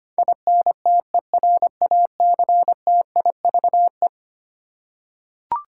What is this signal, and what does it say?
Morse code, heard as a keyed tone, about 25 wpm, 700 Hz. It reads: INTERACTIVE
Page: morse-code-ninja-practice-25wpm